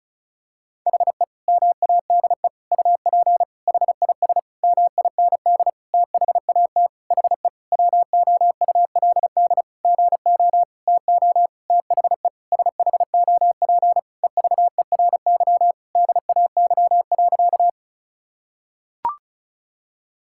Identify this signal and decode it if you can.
Morse code, heard as a keyed tone, about 34 wpm, 700 Hz. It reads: HE MADE UP HIS MIND THAT HE WOULD GO TO THE SHOP EVERY DAY.
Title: Sentences from Top 1600 Words 35wpm